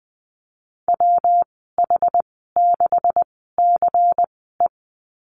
Morse code, heard as a keyed tone, about 20 wpm, spaced as for 15 wpm, 700 Hz. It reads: WH6CE
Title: 173 Lesson 28g - Number 9 - Call Signs 20wpm